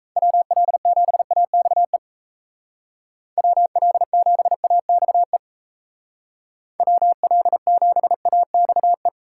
Morse code, heard as a keyed tone, about 33 wpm, 700 Hz. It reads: WL7AXE WL7AXE WL7AXE
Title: Speed Racing - USA 2x3 Call Signs 28wpm